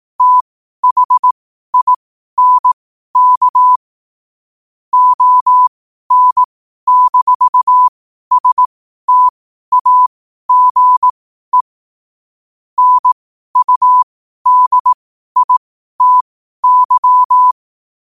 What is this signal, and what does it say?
Morse code, heard as a keyed tone, about 18 wpm, spaced as for 12 wpm, 1000 Hz. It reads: THINK ON-STAGE NUDITY
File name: qotdfiles / 12wpm